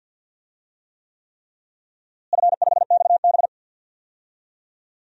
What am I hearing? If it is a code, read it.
Morse code, heard as a keyed tone, about 50 wpm, 700 Hz. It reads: F5XB
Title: DXCC-100 Call Signs - 4 Characters - Repeated 3x - 50wpm